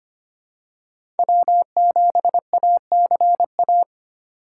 Morse code, heard as a keyed tone, about 25 wpm, 700 Hz. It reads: W7ACA
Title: USA Call Signs - 4x Character Spacing - 25wpm